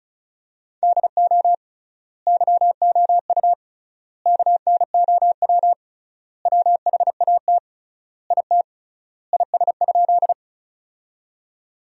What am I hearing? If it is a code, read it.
Morse code, heard as a keyed tone, about 36 wpm, 700 Hz. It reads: DO YOU KNOW WHAT IT IS?
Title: 3x Word-Spacing - Sentences from Top 100 Words - 35wpm